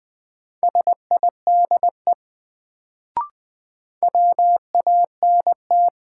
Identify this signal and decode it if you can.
Morse code, heard as a keyed tone, about 20 wpm, 700 Hz. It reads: SIDE WANT
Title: Sets of 2 Words - New Word Drill - 20wpm